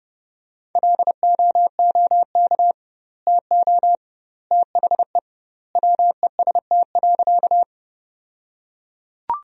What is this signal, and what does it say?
Morse code, heard as a keyed tone, about 30 wpm, 700 Hz. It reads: LOOK TO THE WEST.
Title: 2x Word-Spacing - Sentences from Top 400 Words - 30wpm